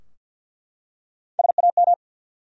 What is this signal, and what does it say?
Morse code, heard as a keyed tone, about 51 wpm, 700 Hz. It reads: SAM
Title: USA Common Names - Max 4 Letters - 50wpm